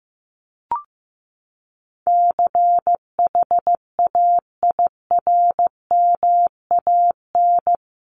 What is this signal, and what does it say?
Morse code, heard as a keyed tone, about 15 wpm, 700 Hz. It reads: CHAIRMAN
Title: Sets of 3 Words 15wpm